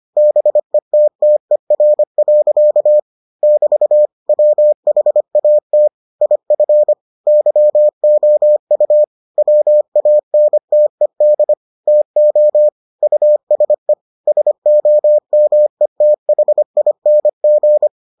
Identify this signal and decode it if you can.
Morse code, heard as a keyed tone, about 25 wpm, 600 Hz. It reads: BETTER. <BT> WHAT IF YOU WANTED TO USE SOMETHING